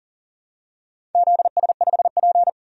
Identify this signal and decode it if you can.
Morse code, heard as a keyed tone, about 39 wpm, 700 Hz. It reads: ZS5P
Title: DXCC-100 Call Signs - 4 Characters - 40wpm